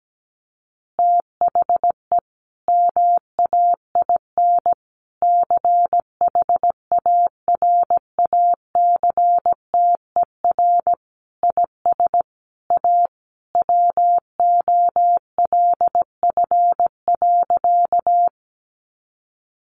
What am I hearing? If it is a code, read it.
Morse code, heard as a keyed tone, about 17 wpm, 700 Hz. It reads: THE MAIN CHARACTER IS A WOLF.
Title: Additional Easy Phrases 17wpm